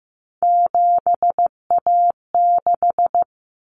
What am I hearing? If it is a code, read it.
Morse code, heard as a keyed tone, about 15 wpm, 700 Hz. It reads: 7A6
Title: Common Cut Numbers - 3-Digit 15wpm